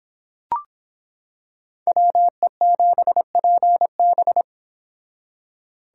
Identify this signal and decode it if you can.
Morse code, heard as a keyed tone, about 26 wpm, 700 Hz. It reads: WE7PB